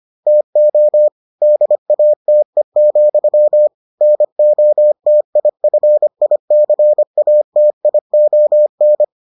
Morse code, heard as a keyed tone, about 25 wpm, 600 Hz. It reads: TO DATE, NOTIFICATION